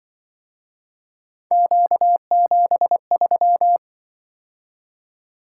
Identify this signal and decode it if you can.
Morse code, heard as a keyed tone, about 25 wpm, 700 Hz. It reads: Q73